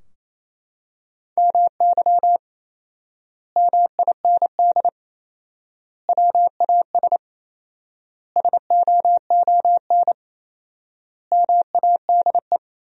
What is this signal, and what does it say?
Morse code, heard as a keyed tone, about 28 wpm, 700 Hz. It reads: MY MIND WAS SOON MADE
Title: 4x Word-Spacing - Sentences from Top 500 Words - 28wpm